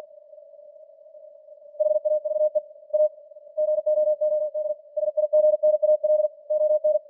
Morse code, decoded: HAVE A GOOD SIGNAL ON